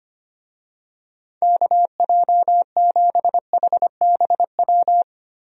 Morse code, heard as a keyed tone, about 25 wpm, 700 Hz. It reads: KJ7HBW
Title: USA Call Signs - 6 Characters - Repeated 3x - 25wpm